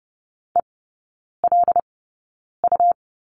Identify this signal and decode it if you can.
Morse code, heard as a keyed tone, about 30 wpm, spaced as for 9 wpm, 690 Hz. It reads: ELU
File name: Lesson 17e - Letter u - Three Characters 30wpm